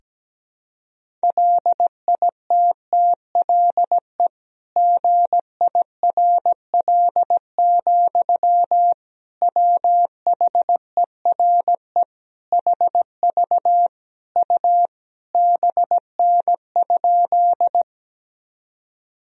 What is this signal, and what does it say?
Morse code, heard as a keyed tone, about 17 wpm, 700 Hz. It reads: LITTLE GIRL, WHERE HV U BN?
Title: Sentences from Top 400 Words and Abbreviations 17wpm